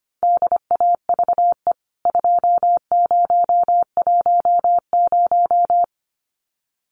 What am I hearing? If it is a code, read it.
Morse code, heard as a keyed tone, about 25 wpm, 700 Hz. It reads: DAVE 2010
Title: CWT Exchange 25wpm